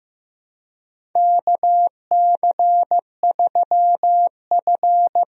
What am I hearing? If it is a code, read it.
Morse code, heard as a keyed tone, about 15 wpm, 700 Hz. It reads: KC3F
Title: USA Call Signs - 4 Characters - Repeated 3x - 15wpm